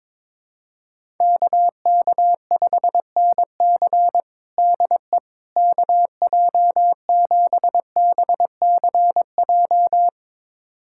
Morse code, heard as a keyed tone, about 22 wpm, 700 Hz. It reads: KK5NC DE KJ7BCJ
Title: USA Callsign DE Callsign 22wpm